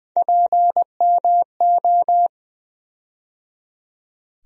Morse code, heard as a keyed tone, about 20 wpm, 700 Hz. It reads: PMO
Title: USA Call Sign Postfixes - 3 Letters 20wpm